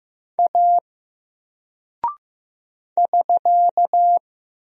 Morse code, heard as a keyed tone, about 15 wpm, 700 Hz. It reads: A <SK>
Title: Single Character (Letter:Number:Punctuation:Prosign) Alt 15wpm